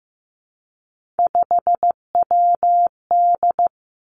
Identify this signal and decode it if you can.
Morse code, heard as a keyed tone, about 15 wpm, 700 Hz. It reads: 5WD
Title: Three Challenging Letter-Number Combinations 15wpm